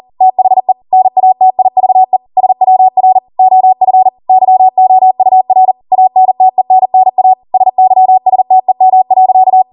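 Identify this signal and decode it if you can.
Morse code, heard as a keyed tone, about 40 wpm, 750 Hz. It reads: THE NATIVE SWR OF YOUR ANTENNA SYSTEM.
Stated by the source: Code practice at 40 wpm.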